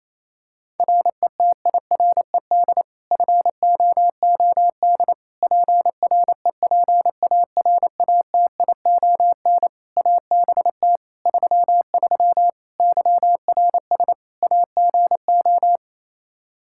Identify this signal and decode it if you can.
Morse code, heard as a keyed tone, about 27 wpm, 700 Hz. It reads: RETIRED FOOD PREPARATION ABT 33 YRS AGO
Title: K1USN SST QSO Practice 28wpm